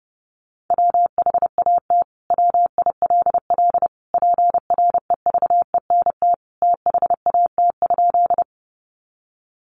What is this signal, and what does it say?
Morse code, heard as a keyed tone, about 30 wpm, 700 Hz. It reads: WHAT WILL PREVENT THAT?
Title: Sentences from Top 1100 Words 30wpm